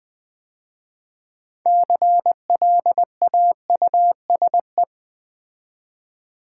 Morse code, heard as a keyed tone, about 20 wpm, 700 Hz. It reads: CLAUSE
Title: 6 Letter Words - First 3 Letters Given - 20wpm